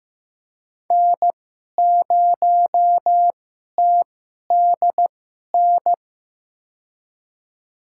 Morse code, heard as a keyed tone, about 15 wpm, spaced as for 11 wpm, 700 Hz. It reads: N0TDN
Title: USA Call Signs - 5 Characters - 2x Character Spacing - 15wpm